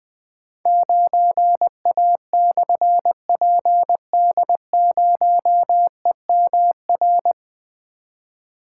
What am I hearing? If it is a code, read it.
Morse code, heard as a keyed tone, about 20 wpm, 700 Hz. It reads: 9A/PD0EMR
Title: Difficult Call Signs 20wpm